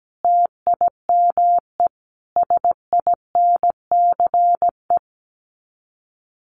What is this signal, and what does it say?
Morse code, heard as a keyed tone, about 17 wpm, 700 Hz. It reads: TIME SINCE
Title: Sets of 5 Words 17wpm